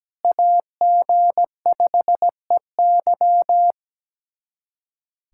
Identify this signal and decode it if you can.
Morse code, heard as a keyed tone, about 17 wpm, 700 Hz. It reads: AG5EY
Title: USA Callsign DE Callsign - 5 Characters - 17wpm